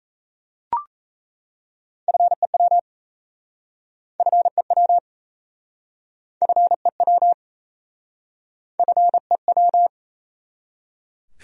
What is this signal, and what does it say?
Morse code, heard as a keyed tone, about 37 wpm, 700 Hz. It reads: FEW FEW FEW FEW